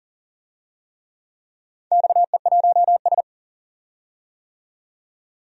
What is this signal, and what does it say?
Morse code, heard as a keyed tone, about 42 wpm, 700 Hz. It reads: XE1S
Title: DXCC-100 Call Signs - 4 Characters - 40wpm